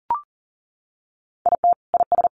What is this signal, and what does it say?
Morse code, heard as a keyed tone, about 40 wpm, 700 Hz. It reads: IT IS
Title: Sentences from Top 1400 Words 40wpm